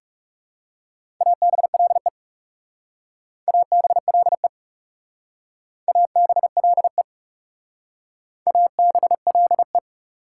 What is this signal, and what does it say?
Morse code, heard as a keyed tone, about 36 wpm, 700 Hz. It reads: ABLE ABLE ABLE ABLE
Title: Speed-Racing Prefixes and Suffixes 30wpm